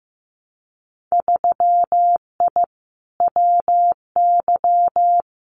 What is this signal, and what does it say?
Morse code, heard as a keyed tone, about 15 wpm, 700 Hz. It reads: 3I WY
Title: Winter Field Day Exchanges Encore 15wpm